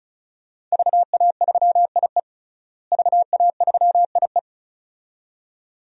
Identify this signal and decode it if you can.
Morse code, heard as a keyed tone, about 36 wpm, 700 Hz. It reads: VA3IE VA3IE
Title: DXCC-100 Call Signs - 5 Characters - Repeated 2x - 35wpm